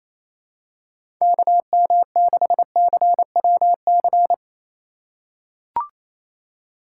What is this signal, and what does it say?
Morse code, heard as a keyed tone, about 28 wpm, 690 Hz. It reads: KM6CWC
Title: USA Call Signs - 6 Characters - 4x Character Spacing - 28wpm